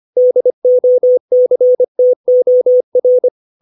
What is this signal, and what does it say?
Morse code, heard as a keyed tone, about 25 wpm, 500 Hz. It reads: DOCTOR
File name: 632 words / 632 Words 25WPM